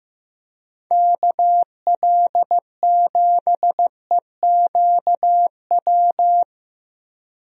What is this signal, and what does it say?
Morse code, heard as a keyed tone, about 16 wpm, 700 Hz. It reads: KL7EQW
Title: DXCC-100 Call Signs - 6 Characters - Repeated 3x - 15wpm